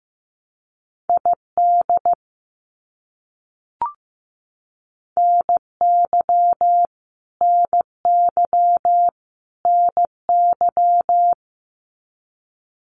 Morse code, heard as a keyed tone, about 15 wpm, 700 Hz. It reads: ID NY NY NY
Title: US States Repeated 3x 15wpm